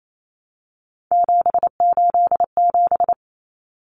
Morse code, heard as a keyed tone, about 28 wpm, 700 Hz. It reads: 787
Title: Common Cut Numbers - 3-Digit 28wpm